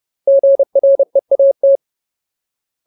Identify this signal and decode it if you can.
Morse code, heard as a keyed tone, about 30 wpm, 550 Hz. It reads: GREAT
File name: Ultimate Word List / 30WPM Ultimate